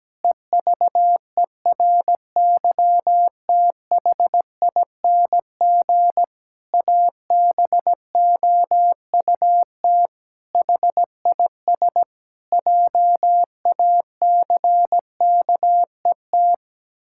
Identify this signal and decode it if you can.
Morse code, heard as a keyed tone, about 17 wpm, 700 Hz. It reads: EVERYTHING ABOUT HIS JACKET
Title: Pangram Challenge v2 - 17wpm